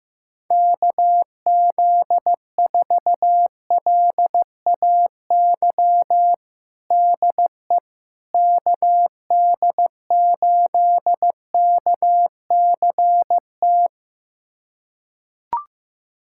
Morse code, heard as a keyed tone, about 15 wpm, 700 Hz. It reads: KZ4LAY DE KD8KCT
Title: USA Callsign DE Callsign - 6 Characters - 15wpm